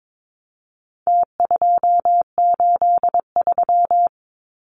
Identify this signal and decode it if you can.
Morse code, heard as a keyed tone, about 22 wpm, 700 Hz. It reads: T283
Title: Numbers - Contest Style - 4-Digit 22wpm